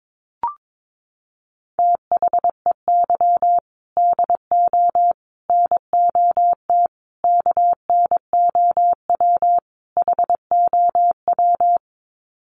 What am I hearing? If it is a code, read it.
Morse code, heard as a keyed tone, about 22 wpm, 700 Hz. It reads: THEY DO NOT KNOW HOW
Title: Sets of 5 Words Encore2 22wpm